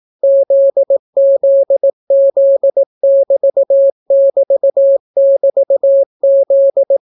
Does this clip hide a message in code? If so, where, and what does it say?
Yes: Morse code from 0.2 to 7.0 s, heard as a keyed tone, about 18 wpm, 550 Hz. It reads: ZZZ<BT><BT><BT>Z